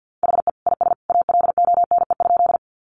Morse code, heard as a keyed tone, about 50 wpm, 700 Hz. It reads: HE IS ALONE?
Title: Sentences from Top 1000 Words 50wpm